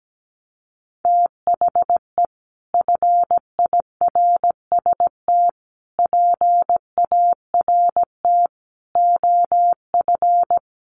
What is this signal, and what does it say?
Morse code, heard as a keyed tone, about 17 wpm, 700 Hz. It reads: THE FIRST PART OF
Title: Sets of 5 Words Encore 17wpm